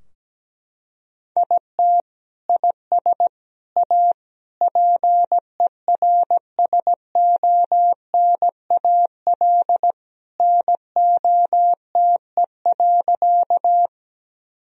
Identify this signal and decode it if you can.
Morse code, heard as a keyed tone, about 17 wpm, 700 Hz. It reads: IT IS A PERSONAL NOTE.
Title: Sentences from Top 1000 Words - Review Part 4 - 17wpm